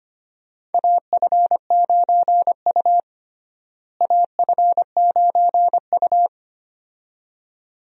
Morse code, heard as a keyed tone, about 25 wpm, 700 Hz. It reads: AF9U AF9U
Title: USA Call Signs - 4 Characters - Repeated 2x - 25wpm